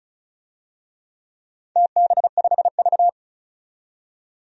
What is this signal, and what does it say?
Morse code, heard as a keyed tone, about 36 wpm, 700 Hz. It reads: TB5V
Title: DXCC-100 Call Signs - 4 Characters - 35wpm